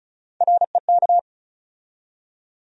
Morse code, heard as a keyed tone, about 34 wpm, 700 Hz. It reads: REK
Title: USA Call Sign Postfixes - 3 Letters 35wpm